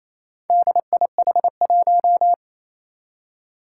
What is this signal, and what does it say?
Morse code, heard as a keyed tone, about 29 wpm, 700 Hz. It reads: DIH1
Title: Four Challenging Letter-Number Combinations 28wpm